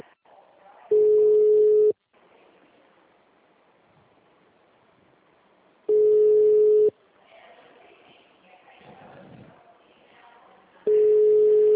Ringback tone (Europe, 425 Hz).